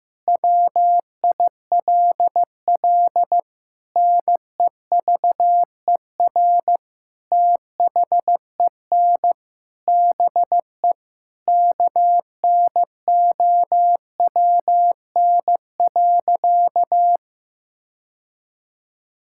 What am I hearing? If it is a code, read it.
Morse code, heard as a keyed tone, about 15 wpm, 700 Hz. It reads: WILL NEVER THEN BE KNOWN.
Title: Sentences from Top 1600 Words 15wpm